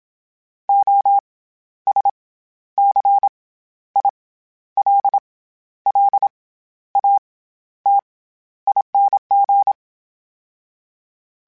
Morse code, heard as a keyed tone, about 27 wpm, 800 Hz. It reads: O S C I L L A T ING
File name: Special INGs-mp3s / Special INGs-mp3s / SlowToFaster ing @25